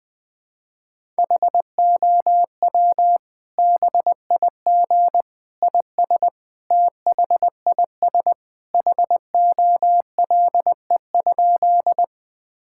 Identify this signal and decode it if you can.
Morse code, heard as a keyed tone, about 20 wpm, 700 Hz. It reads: HOW BIG IS THIS HOLE?
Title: Sentences from Top 1600 Words 20wpm